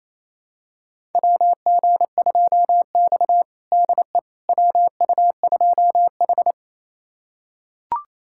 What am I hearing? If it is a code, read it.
Morse code, heard as a keyed tone, about 28 wpm, 700 Hz. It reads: WG2X DE WU2H
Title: USA Callsign DE Callsign - 4 Characters - 28wpm